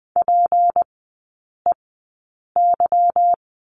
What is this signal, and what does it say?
Morse code, heard as a keyed tone, about 20 wpm, spaced as for 8 wpm, 700 Hz. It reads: PEY
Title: 152 Lesson 25e - Letter p - Three Characters 20wpm